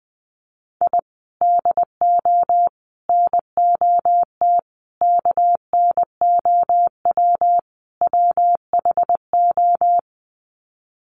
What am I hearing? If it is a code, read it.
Morse code, heard as a keyed tone, about 20 wpm, 700 Hz. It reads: I DO NOT KNOW WHO
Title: Sets of 5 Words 20wpm